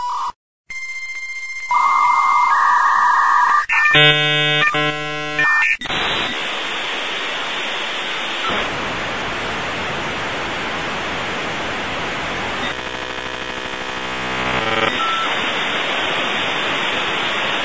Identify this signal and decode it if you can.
Dial-up modem handshake.